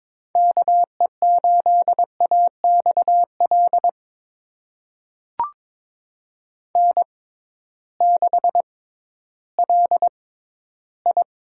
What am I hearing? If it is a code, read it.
Morse code, heard as a keyed tone, about 22 wpm, spaced as for 13 wpm, 700 Hz. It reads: KE8AXL N6LI
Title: USA Call Signs - 6x Character Spacing - 22wpm